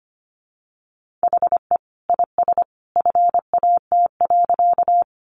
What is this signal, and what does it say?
Morse code, heard as a keyed tone, about 25 wpm, 700 Hz. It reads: HE IS FAT.
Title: Sentences from Top 600 Words 25wpm